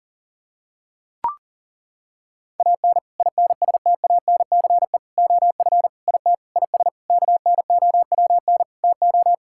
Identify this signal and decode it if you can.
Morse code, heard as a keyed tone, about 40 wpm, 700 Hz. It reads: AN INSTANCE OF IT IS KNOWN TO
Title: Sentences from Top 1000 Words - Review Part 5 - 40wpm